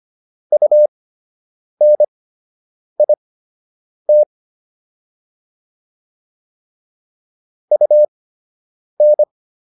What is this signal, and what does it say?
Morse code, heard as a keyed tone, about 25 wpm, spaced as for 8 wpm, 600 Hz. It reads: UNIT UN